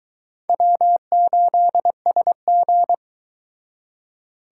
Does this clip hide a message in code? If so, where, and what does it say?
Yes: Morse code from 0.5 to 2.9 s, heard as a keyed tone, about 22 wpm, 700 Hz. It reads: W8SG